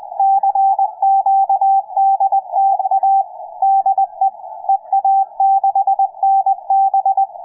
Morse code, heard as a keyed tone, about 19 wpm, 780 Hz, with radio noise. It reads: CQDTA DE EA6NB
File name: EA6NB-160-CW-4ABR15